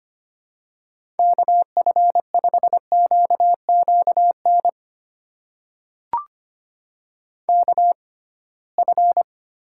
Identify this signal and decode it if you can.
Morse code, heard as a keyed tone, about 25 wpm, spaced as for 19 wpm, 700 Hz. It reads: KF5QQN KF
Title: USA Call Signs - 6 Characters - 6x Character Spacing - 25wpm